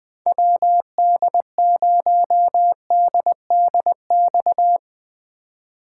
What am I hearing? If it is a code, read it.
Morse code, heard as a keyed tone, about 20 wpm, 700 Hz. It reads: WD0DDX
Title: USA Call Signs - 6 Characters - 4x Character Spacing - 20wpm